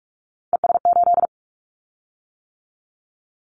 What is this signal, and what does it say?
Morse code, heard as a keyed tone, about 44 wpm, 700 Hz. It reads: ES8
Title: Callsign Prefixes for Top 100 DXCC - 45wpm